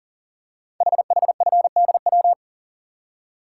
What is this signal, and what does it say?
Morse code, heard as a keyed tone, about 42 wpm, 700 Hz. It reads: HHFDW
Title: Code Groups - 5 Letters - 40wpm